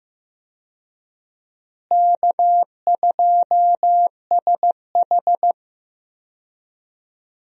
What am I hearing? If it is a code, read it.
Morse code, heard as a keyed tone, about 15 wpm, 700 Hz. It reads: K2SH
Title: USA Call Signs - 4 Characters - Repeated 3x - 15wpm